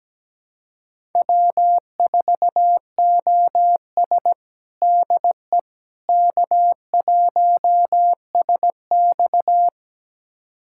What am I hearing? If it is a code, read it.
Morse code, heard as a keyed tone, about 17 wpm, 700 Hz. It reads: W4OS DE K1SX
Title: USA Callsign DE Callsign - 4 Characters - 17wpm